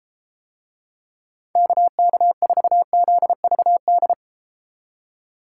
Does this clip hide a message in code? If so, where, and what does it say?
Yes: Morse code from 1.6 to 4.1 s, heard as a keyed tone, about 33 wpm, 700 Hz. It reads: KK4ZVD